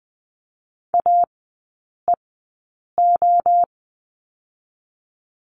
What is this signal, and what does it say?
Morse code, heard as a keyed tone, about 20 wpm, 700 Hz. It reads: A E O
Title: 023 Lesson 5e - Letter o - Three Characters 20wpm